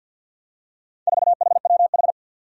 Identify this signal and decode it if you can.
Morse code, heard as a keyed tone, about 49 wpm, 700 Hz. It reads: 4HWH
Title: Four Challenging Letter-Number Combinations 50wpm